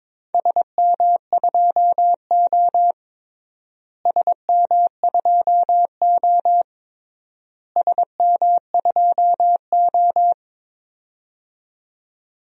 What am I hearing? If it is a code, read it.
Morse code, heard as a keyed tone, about 22 wpm, 700 Hz. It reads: SM2O SM2O SM2O
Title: DXCC-100 Call Signs - 4 Characters - Repeated 3x - 22wpm